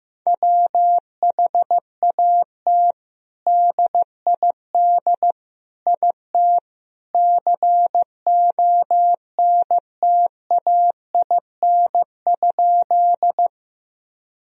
Morse code, heard as a keyed tone, about 15 wpm, 700 Hz. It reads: WHAT DID IT CONTAIN?
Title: Sentences from Top 1000 Words - Review Part 2 - 15wpm